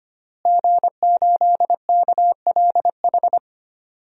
Morse code, heard as a keyed tone, about 25 wpm, 700 Hz. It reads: G8KLH
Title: DXCC-100 Call Signs - 5 Characters - Repeated 3x - 25wpm